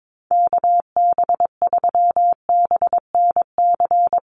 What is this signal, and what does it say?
Morse code, heard as a keyed tone, about 22 wpm, 700 Hz. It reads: KB3BNC
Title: USA Call Signs Repeated 3x 22wpm